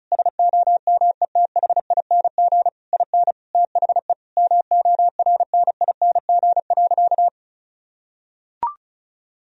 Morse code, heard as a keyed tone, about 36 wpm, 700 Hz. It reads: SOMETHING IN THE MORNING.
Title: Sentences from Top 1000 Words - Review Part 4 - 35wpm